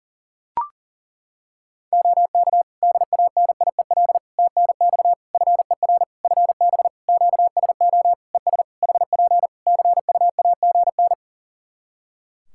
Morse code, heard as a keyed tone, about 40 wpm, 700 Hz. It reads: OK DANIEL TNX FER FB QSO ES HP CUAGN
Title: K1USN SST QSO Practice 40wpm